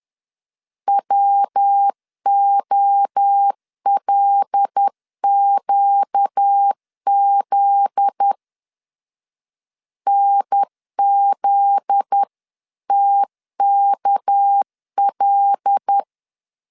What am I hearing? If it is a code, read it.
Morse code, heard as a keyed tone, about 10 wpm, 800 Hz. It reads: WOLQZ NZTKL
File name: LEZIONI / 7 GF9Z / 7 GF9Z 10wpm 2611